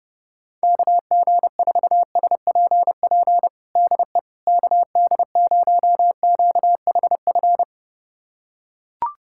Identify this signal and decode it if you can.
Morse code, heard as a keyed tone, about 30 wpm, 700 Hz. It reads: KG4SPP DE KD0QHF
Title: USA Callsign DE Callsign - 6 Characters - 30wpm